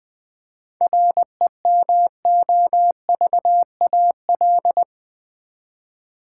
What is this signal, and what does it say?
Morse code, heard as a keyed tone, about 20 wpm, 700 Hz. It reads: REMOVAL
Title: 7 Letter Words - First 3 Letters Given - 20wpm